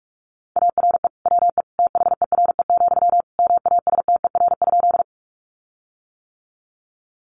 Morse code, heard as a keyed tone, about 44 wpm, 700 Hz. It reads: ARE WE THERE, MASTER?
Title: Sentences from Top 1300 Words 45wpm